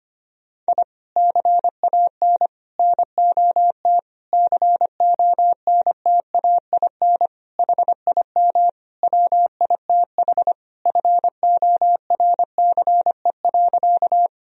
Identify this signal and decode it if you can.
Morse code, heard as a keyed tone, about 25 wpm, 700 Hz. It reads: I CAN NOT CONTAIN HIM WITH FORCE.
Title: Sentences from Top 1000 Words - Review Part 3 - 25wpm